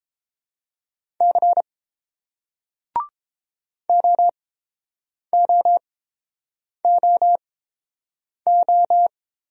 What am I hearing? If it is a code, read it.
Morse code, heard as a keyed tone, about 31 wpm, spaced as for 8 wpm, 700 Hz. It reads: C OOOO